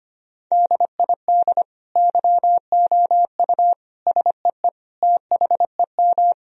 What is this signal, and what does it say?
Morse code, heard as a keyed tone, about 25 wpm, 700 Hz. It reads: DID YOU SEE THEM
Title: Sentences from Top 1000 Words - Review Part 2 - 25wpm